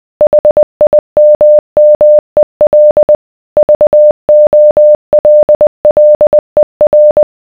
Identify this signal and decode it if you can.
Morse code, heard as a keyed tone, about 20 wpm, 600 Hz. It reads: HIMMEL VOLLER